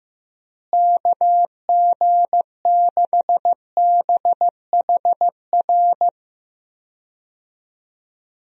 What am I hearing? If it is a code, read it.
Morse code, heard as a keyed tone, about 15 wpm, 700 Hz. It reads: KG6BHR
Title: USA Call Signs - 6 Characters - Repeated 3x - 15wpm